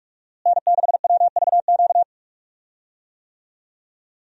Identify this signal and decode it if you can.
Morse code, heard as a keyed tone, about 44 wpm, 700 Hz. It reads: N6WVQ
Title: USA Call Signs - 5 Characters - Repeated 3x - 45wpm